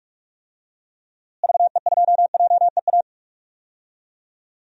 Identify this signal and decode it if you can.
Morse code, heard as a keyed tone, about 42 wpm, 700 Hz. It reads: VE2JEA
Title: DXCC-100 Call Signs - 6 Characters - Repeated 2x - 45wpm